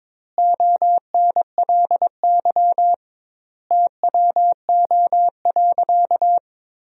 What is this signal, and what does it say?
Morse code, heard as a keyed tone, about 22 wpm, 700 Hz. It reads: ONLY TWO.
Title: 2x Word-Spacing - Sentences from Top 200 Words - 22wpm